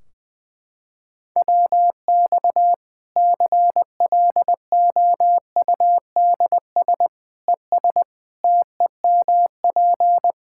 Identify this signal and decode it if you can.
Morse code, heard as a keyed tone, about 21 wpm, 700 Hz. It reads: WX CLOUDS ES TEMP
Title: Practice QSO - Exchange 2 - Info - 20wpm